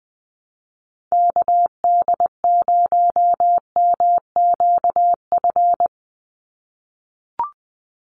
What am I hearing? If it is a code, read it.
Morse code, heard as a keyed tone, about 20 wpm, 700 Hz. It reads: KD0MQF
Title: USA Call Signs 20wpm